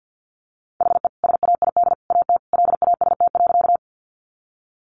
Morse code, heard as a keyed tone, about 50 wpm, 700 Hz. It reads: HE SAID AT LAST.
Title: Sentences from Top 300 Words 50wpm